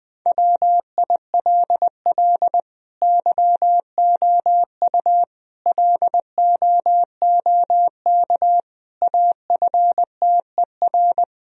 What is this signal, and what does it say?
Morse code, heard as a keyed tone, about 21 wpm, 700 Hz. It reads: WILL YOU LOOK AFTER
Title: Sentences from Top 1000 Words - Review Part 1 - 20wpm